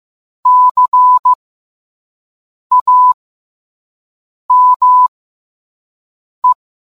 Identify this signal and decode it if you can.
Morse code, heard as a keyed tone, about 15 wpm, spaced as for 5 wpm, 1000 Hz. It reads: CAME